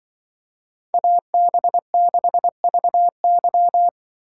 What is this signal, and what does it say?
Morse code, heard as a keyed tone, about 24 wpm, 700 Hz. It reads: AB6VY
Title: Speed Racing - USA 2x2 Call Signs 20wpm